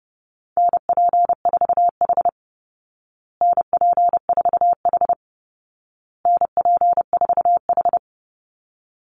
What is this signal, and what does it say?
Morse code, heard as a keyed tone, about 30 wpm, 700 Hz. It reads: NP4H NP4H NP4H
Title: Call Signs Repeated 2x - 30wpm